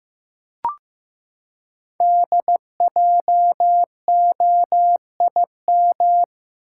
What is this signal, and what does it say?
Morse code, heard as a keyed tone, about 15 wpm, 700 Hz. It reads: DJOIM
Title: Code Groups - 5 Letters - 15wpm